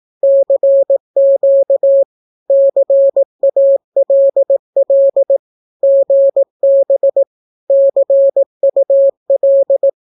Morse code, heard as a keyed tone, about 18 wpm, 550 Hz. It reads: CQ CALL GB CUL
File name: naukapodstaw / skroty / skroty